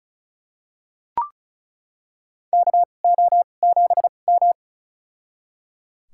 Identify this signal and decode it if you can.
Morse code, heard as a keyed tone, about 34 wpm, spaced as for 25 wpm, 700 Hz. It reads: KO7M
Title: USA Call Signs - 4 Characters - 2x Character Spacing - 35wpm